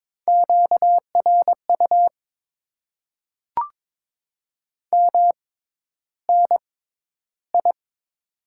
Morse code, heard as a keyed tone, about 21 wpm, spaced as for 11 wpm, 700 Hz. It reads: QRU MNI
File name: Top 90 QSO Elements - 6x Character Spacing - 22wpm